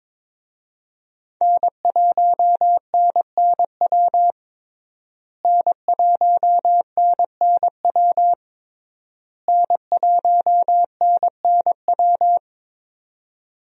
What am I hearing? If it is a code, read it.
Morse code, heard as a keyed tone, about 22 wpm, 700 Hz. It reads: N1NNW N1NNW N1NNW
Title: USA Call Signs - 5 Characters - Repeated 3x - 22wpm